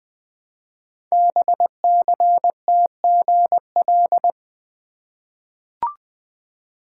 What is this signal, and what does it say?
Morse code, heard as a keyed tone, about 20 wpm, 700 Hz. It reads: BCTGL
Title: Code Groups - 5 Letters - 20wpm